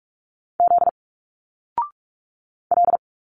Morse code, heard as a keyed tone, about 42 wpm, 700 Hz. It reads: Z L
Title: Single Letters 45wpm